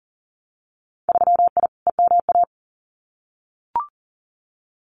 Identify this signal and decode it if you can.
Morse code, heard as a keyed tone, about 40 wpm, 700 Hz. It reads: 3I EMA
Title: Winter Field Day Exchanges Encore 40wpm